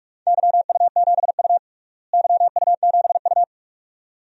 Morse code, heard as a keyed tone, about 46 wpm, 700 Hz. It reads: YU7U YU7U
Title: DXCC-100 Call Signs - 4 Characters - Repeated 2x - 45wpm